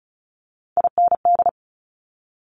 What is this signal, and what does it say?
Morse code, heard as a keyed tone, about 36 wpm, 700 Hz. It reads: IND
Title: Top Three Letter Combinations in Words 35wpm